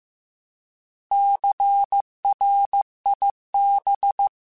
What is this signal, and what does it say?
Morse code, heard as a keyed tone, about 15 wpm, 800 Hz. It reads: CRIB